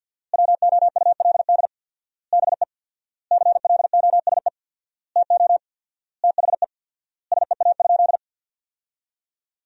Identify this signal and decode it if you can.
Morse code, heard as a keyed tone, about 50 wpm, 700 Hz. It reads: WOULD BE CLOSE TO THE SEA?
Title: 4x Word-Spacing - Sentences from Top 300 Words - 50wpm